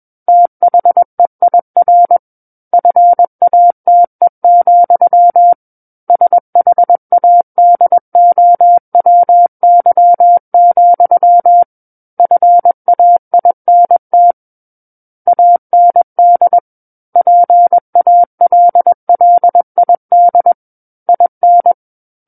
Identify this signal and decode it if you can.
Morse code, heard as a keyed tone, about 21 wpm, 700 Hz. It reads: THEIR FATE, SHADOWY, FAINT AND PALLID IN
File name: Book One / 12-Chapter 12 - 21 WPM